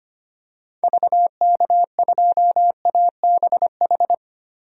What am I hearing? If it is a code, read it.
Morse code, heard as a keyed tone, about 25 wpm, 700 Hz. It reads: VK2ABH
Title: DXCC-100 Call Signs - 6 Characters - Repeated 2x - 25wpm